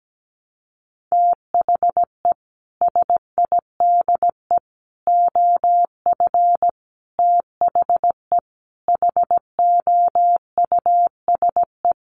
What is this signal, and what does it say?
Morse code, heard as a keyed tone, about 17 wpm, 700 Hz. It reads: THE SIDE OF THE HOUSE
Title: Sets of 5 Words Encore2 17wpm